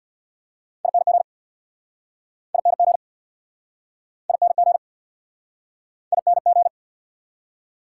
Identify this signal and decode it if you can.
Morse code, heard as a keyed tone, about 69 wpm, 690 Hz. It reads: ING ING ING ING